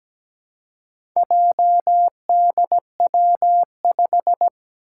Morse code, heard as a keyed tone, about 17 wpm, 700 Hz. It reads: JDW5
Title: Four Challenging Letter-Number Combinations 17wpm